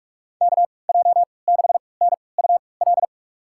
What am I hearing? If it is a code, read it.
Morse code, heard as a keyed tone, about 46 wpm, spaced as for 26 wpm, 700 Hz. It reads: KJ6NUL
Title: USA Call Signs - 6 Characters - 3x Character Spacing - 45wpm